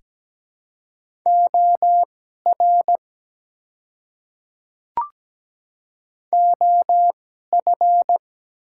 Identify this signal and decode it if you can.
Morse code, heard as a keyed tone, about 17 wpm, spaced as for 12 wpm, 690 Hz. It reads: OR OF
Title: 2x Character Spacing - Top 100 Words - 17wpm